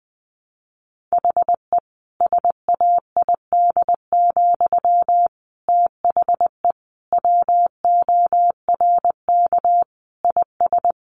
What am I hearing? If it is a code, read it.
Morse code, heard as a keyed tone, about 20 wpm, 700 Hz. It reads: HE SAID, THE WORK IS
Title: Sentences from Top 400 Words 20wpm